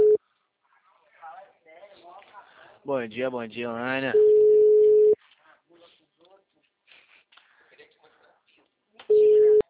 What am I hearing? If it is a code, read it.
Ringback tone (Europe, 425 Hz), with speech.